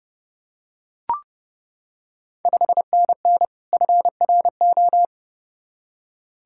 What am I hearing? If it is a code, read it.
Morse code, heard as a keyed tone, about 28 wpm, 690 Hz. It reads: 5NN FRO
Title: Ontario QSO Party Exchange 30wpm